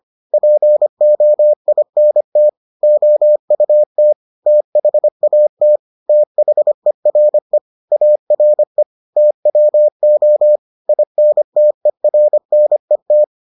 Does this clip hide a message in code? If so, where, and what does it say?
Yes: Morse code from 0.3 to 13.2 s, heard as a keyed tone, about 25 wpm, 600 Hz. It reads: POINT OUT THAT THERE ARE TWO INTERNET